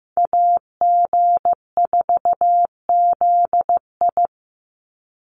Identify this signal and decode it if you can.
Morse code, heard as a keyed tone, about 15 wpm, 700 Hz. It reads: AG4ZI
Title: USA Call Signs 15wpm